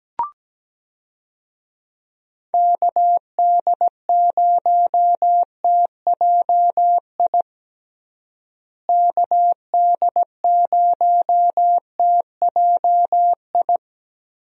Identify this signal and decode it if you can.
Morse code, heard as a keyed tone, about 17 wpm, 700 Hz. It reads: KD0TJI KD0TJI
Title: USA Call Signs - 6 Characters - Repeated 3x - 17wpm